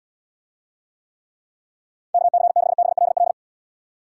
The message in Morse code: KC6BLL

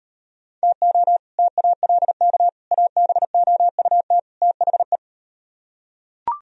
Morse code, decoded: TO TALK ABOUT THE

